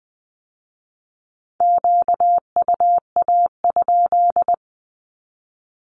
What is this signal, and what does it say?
Morse code, heard as a keyed tone, about 20 wpm, 700 Hz. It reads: QUA?
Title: Q-Codes with Spoken Meaning - 20wpm